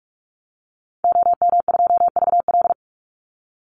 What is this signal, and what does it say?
Morse code, heard as a keyed tone, about 44 wpm, 700 Hz. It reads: OM2VL
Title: Call Signs Repeated 3x - 45wpm